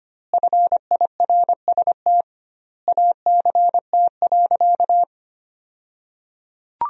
Morse code, heard as a keyed tone, about 25 wpm, 700 Hz. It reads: FIRST ACT.
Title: 2x Word-Spacing - Sentences from Top 200 Words - 25wpm